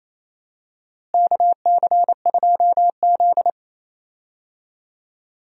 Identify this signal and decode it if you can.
Morse code, heard as a keyed tone, about 28 wpm, 700 Hz. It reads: KC2Z
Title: USA Call Signs - 4 Characters - Repeated 2x - 28wpm